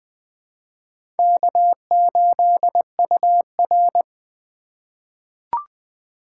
Morse code, heard as a keyed tone, about 20 wpm, 700 Hz. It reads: K8UR
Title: USA Call Signs - 3x Character Spacing - 20wpm